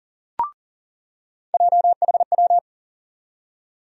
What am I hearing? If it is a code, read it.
Morse code, heard as a keyed tone, about 40 wpm, 700 Hz. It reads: JHW
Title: USA Call Sign Postfixes - 3 Letters 40wpm